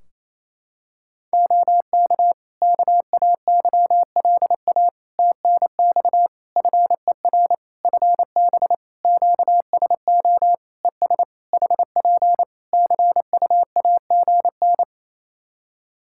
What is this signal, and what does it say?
Morse code, heard as a keyed tone, about 28 wpm, 700 Hz. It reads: OK KAYLA TNX FER FB QSO ES HP CUAGN
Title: K1USN SST QSO Practice 28wpm